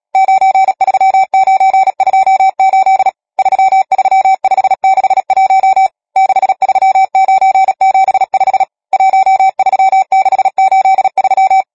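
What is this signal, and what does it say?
Morse code, heard as a keyed tone, about 37 wpm, 770 Hz. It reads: 93928 33561 63975 13683
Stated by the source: Morse code number practice, machine-sent training text (Nácvik telegrafie, OK1FCB).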